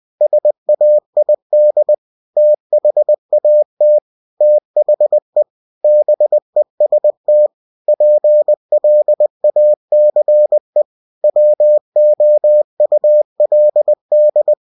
Morse code, heard as a keyed tone, about 20 wpm, 600 Hz. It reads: SAID THAT THE BEST PLACE WOULD
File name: cw-books / winnie-the-pooh / winnie-20wpm-page-0036